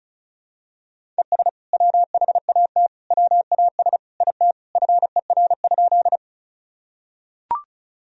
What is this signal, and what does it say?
Morse code, heard as a keyed tone, about 36 wpm, 700 Hz. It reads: ES WHAT WAS IT FER?
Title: Sentences from Top 100 Words and Abbreviations 35wpm